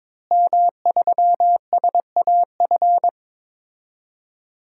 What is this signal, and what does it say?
Morse code, heard as a keyed tone, about 21 wpm, 700 Hz. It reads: M3SAF
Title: DXCC-100 Call Signs - 5 Characters - Repeated 3x - 22wpm